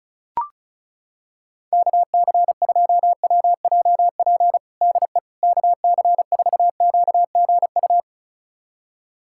Morse code, heard as a keyed tone, about 34 wpm, 700 Hz. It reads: KC2WJP DE KC4QGU
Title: USA Callsign DE Callsign - 6 Characters - 35wpm